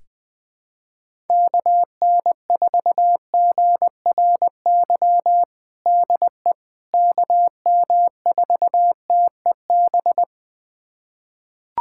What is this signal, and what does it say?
Morse code, heard as a keyed tone, about 20 wpm, 700 Hz. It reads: KN4GRY DE KM4TEB
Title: USA Callsign DE Callsign 20wpm